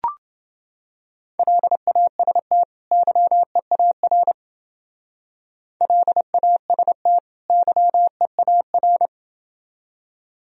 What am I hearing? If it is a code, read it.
Morse code, heard as a keyed tone, about 28 wpm, 700 Hz. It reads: LAST YEAR LAST YEAR